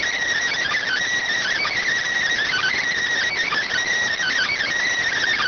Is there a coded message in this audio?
Yes: Emergency Alert System (SAME) tones from 0.6 to 4.5 s, heard as harsh buzzing bursts.